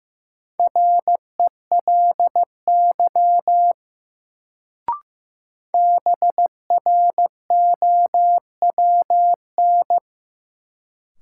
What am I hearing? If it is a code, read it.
Morse code, heard as a keyed tone, about 15 wpm, 700 Hz. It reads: RELY BROWN
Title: Sentences from Top 1200 Words - New Word Drill - 15wpm